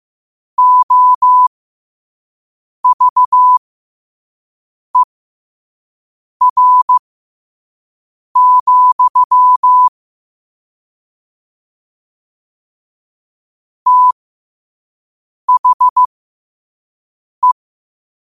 Morse code, heard as a keyed tone, about 15 wpm, spaced as for 5 wpm, 1000 Hz. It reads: OVER, THE
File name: qotdfiles / 5wpm